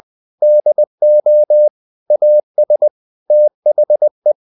Morse code, heard as a keyed tone, about 20 wpm, 600 Hz. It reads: DO AS THE
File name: CW / Practice / EnglishSayings20